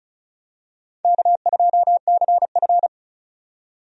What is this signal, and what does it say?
Morse code, heard as a keyed tone, about 36 wpm, 700 Hz. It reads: K2CF
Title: USA Call Signs - 4 Characters - 35wpm